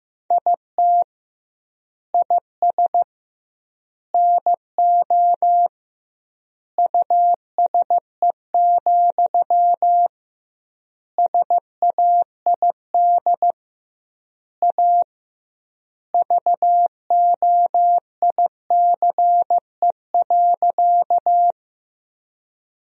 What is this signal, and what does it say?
Morse code, heard as a keyed tone, about 15 wpm, 700 Hz. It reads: IT IS NO USE, SAID A VOICE.
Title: 2x Word-Spacing - Sentences from Top 500 Words - 15wpm